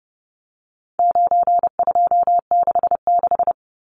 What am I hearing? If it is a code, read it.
Morse code, heard as a keyed tone, about 30 wpm, 700 Hz. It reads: 9266
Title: Numbers - 4-Digit 30wpm